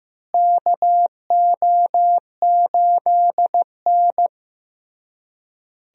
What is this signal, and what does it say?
Morse code, heard as a keyed tone, about 15 wpm, 700 Hz. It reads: KO8N
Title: USA Call Signs - 4 Characters - Repeated 3x - 15wpm